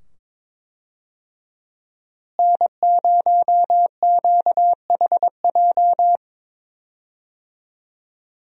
Morse code, heard as a keyed tone, about 22 wpm, 700 Hz. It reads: N0QHJ
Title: USA Call Signs - 5 Characters - Repeated 2x - 22wpm